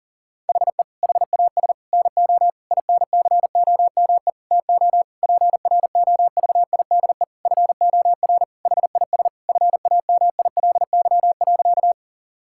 Morse code, heard as a keyed tone, about 40 wpm, 700 Hz. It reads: HE HAS NO INCOME TO PROVIDE FOR HIS FAMILY.
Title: Sentences from Top 1000 Words - Review Part 5 - 40wpm